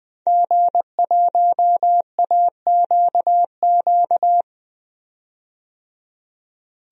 Morse code, heard as a keyed tone, about 20 wpm, 700 Hz. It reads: G1AQQ
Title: DXCC-100 Call Signs - 5 Characters - 20wpm